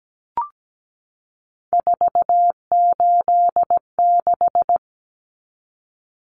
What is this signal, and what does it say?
Morse code, heard as a keyed tone, about 17 wpm, 700 Hz. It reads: 486
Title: Common Cut Numbers - 3-Digit 17wpm